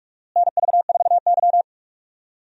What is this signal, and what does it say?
Morse code, heard as a keyed tone, about 42 wpm, 700 Hz. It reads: NV4Y
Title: USA Call Signs - 4 Characters - Repeated 3x - 45wpm